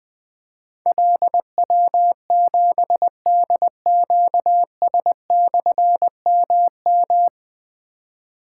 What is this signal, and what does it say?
Morse code, heard as a keyed tone, about 20 wpm, 700 Hz. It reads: LW7DQS/MM
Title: Difficult Call Signs 20wpm